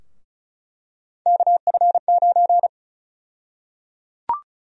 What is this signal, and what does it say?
Morse code, heard as a keyed tone, about 36 wpm, 700 Hz. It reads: KF9
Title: USA Call Sign Prefixes - 35wpm